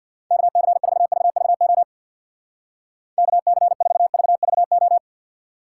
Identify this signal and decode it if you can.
Morse code, heard as a keyed tone, about 62 wpm, 700 Hz. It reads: KC4VVO KC4VVO
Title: Speed Racing - USA 2x3 Call Signs 50wpm